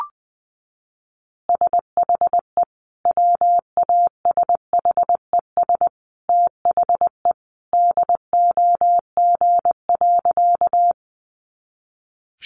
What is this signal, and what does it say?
Morse code, heard as a keyed tone, about 20 wpm, 700 Hz. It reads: SHE WASHES THE DOG.
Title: Easy Phrases 20wpm